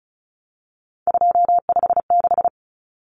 Morse code, heard as a keyed tone, about 34 wpm, 700 Hz. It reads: 256
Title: Common Cut Numbers - 3-Digit 35wpm